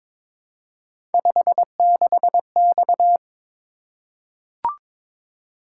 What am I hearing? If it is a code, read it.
Morse code, heard as a keyed tone, about 22 wpm, 700 Hz. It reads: 56X
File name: practice / 22wpm / Three Letter-Number Combinations - 22wpm